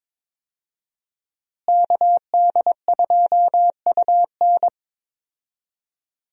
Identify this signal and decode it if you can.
Morse code, heard as a keyed tone, about 22 wpm, 700 Hz. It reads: KD2UN
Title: USA Call Signs - 5 Characters - Repeated 3x - 22wpm